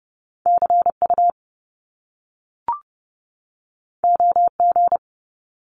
Morse code, heard as a keyed tone, about 30 wpm, 700 Hz. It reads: CU OG
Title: Top Two Letter Combinations in Words 30wpm